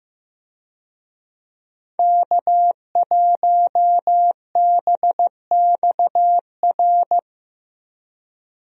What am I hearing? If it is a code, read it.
Morse code, heard as a keyed tone, about 15 wpm, 700 Hz. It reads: K1BXR
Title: USA Call Signs - 5 Characters - Repeated 3x - 15wpm